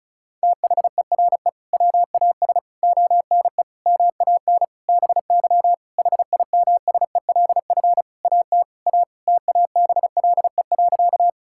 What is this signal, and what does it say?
Morse code, heard as a keyed tone, about 36 wpm, 700 Hz. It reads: THERE WAS ONE MAN BY HIMSELF AT A TABLE.
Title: Sentences from Top 1600 Words 35wpm